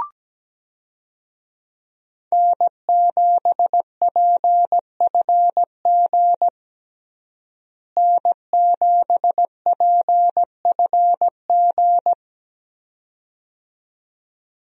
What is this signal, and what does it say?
Morse code, heard as a keyed tone, about 17 wpm, 700 Hz. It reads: N7PFG N7PFG
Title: USA Call Signs - 5 Characters - Repeated 2x - 17wpm